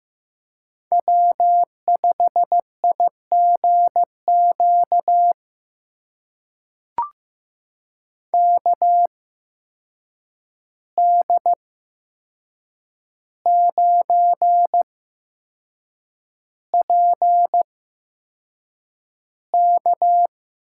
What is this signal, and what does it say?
Morse code, heard as a keyed tone, about 15 wpm, spaced as for 6 wpm, 700 Hz. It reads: W5IGQ KD9PK
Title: USA Call Signs - 5 Characters - 8x Character Spacing - 15wpm